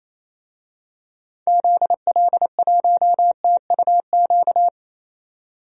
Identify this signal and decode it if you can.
Morse code, heard as a keyed tone, about 28 wpm, 700 Hz. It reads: ZL1TUQ
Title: DXCC-100 Call Signs - 6 Characters - 28wpm